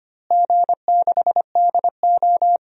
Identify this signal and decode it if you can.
Morse code, heard as a keyed tone, about 25 wpm, 700 Hz. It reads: G6DO
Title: DXCC-100 Call Signs - 4 Characters - Repeated 2x - 25wpm